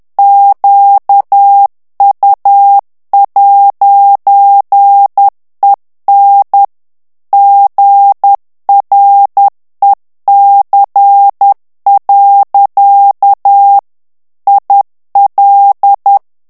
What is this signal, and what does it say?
Morse code, heard as a keyed tone, about 10 wpm, 800 Hz. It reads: QU'EN GREC. IL
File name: morse mp3 / FRANCAIS TEXTE CLAIR 2 VITESSE 10 MOTS MINUTES